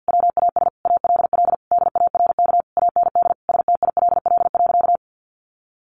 Morse code, heard as a keyed tone, about 50 wpm, 700 Hz. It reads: WAS ALL DARK AND STILL.
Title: Sentences from Top 500 Words 50wpm